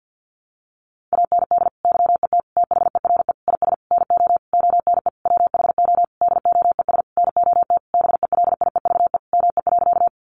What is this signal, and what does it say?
Morse code, heard as a keyed tone, about 48 wpm, 700 Hz. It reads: AND YET THERE IS NO ONE WHO DOES NOT BELIEVE ME.
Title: Sentences from Top 1000 Words 50wpm